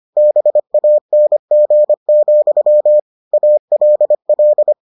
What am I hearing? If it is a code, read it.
Morse code, heard as a keyed tone, about 25 wpm, 600 Hz. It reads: BANG, ALL